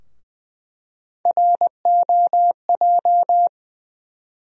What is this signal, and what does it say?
Morse code, heard as a keyed tone, about 20 wpm, 700 Hz. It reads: ROJ
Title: USA Call Sign Postfixes - 3 Letters 20wpm